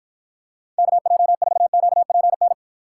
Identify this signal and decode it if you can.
Morse code, heard as a keyed tone, about 54 wpm, 700 Hz. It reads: KJ4QPN